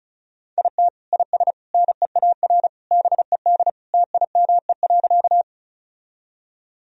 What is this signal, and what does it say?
Morse code, heard as a keyed tone, about 36 wpm, 700 Hz. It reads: IT IS NEAR BED TIME.
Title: Sentences from Top 1000 Words - Review Part 1 - 35wpm